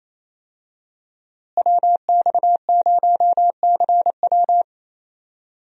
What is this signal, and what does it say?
Morse code, heard as a keyed tone, about 28 wpm, 700 Hz. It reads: WX0CW
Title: USA Call Signs - 5 Characters - Repeated 3x - 28wpm